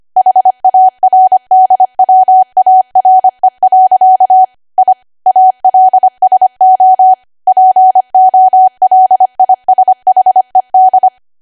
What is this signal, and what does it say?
Morse code, heard as a keyed tone, about 25 wpm, 750 Hz. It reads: HARDWARE. I ALSO POLISHED
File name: Archive / 25 WPM / 25WPM